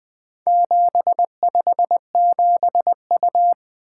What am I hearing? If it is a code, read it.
Morse code, heard as a keyed tone, about 20 wpm, 700 Hz. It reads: 757U
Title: Four Letter-Number Combinations - 20wpm